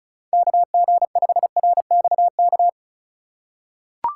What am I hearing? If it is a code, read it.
Morse code, heard as a keyed tone, about 36 wpm, 700 Hz. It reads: KG5RXK
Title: USA Call Signs - 6 Characters - 3x Character Spacing - 35wpm